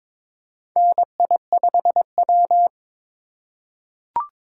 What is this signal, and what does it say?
Morse code, heard as a keyed tone, about 22 wpm, 700 Hz. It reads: NI5W
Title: USA Call Signs - 8x Character Spacing - 22wpm